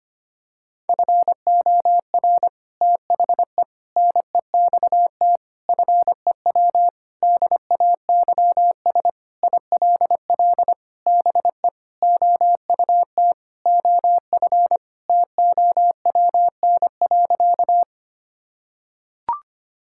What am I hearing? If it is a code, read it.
Morse code, heard as a keyed tone, about 25 wpm, 700 Hz. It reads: FOR THE NEXT FEW DAYS ILL BE OUT OF TOWN.
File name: practice / 25wpm / Sentences with 3-Word Phrases - 25wpm